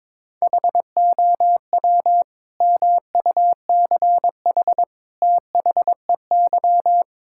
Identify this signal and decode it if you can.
Morse code, heard as a keyed tone, about 22 wpm, 700 Hz. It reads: HOW MUCH THEY
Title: Sentences from Top 1000 Words - Review Part 5 - 22wpm